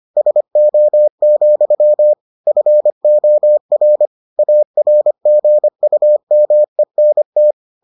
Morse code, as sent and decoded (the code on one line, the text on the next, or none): ... --- --..-- / ..-. --- .-. / .- .-. --. ..- -- . -. -
SO, FOR ARGUMENT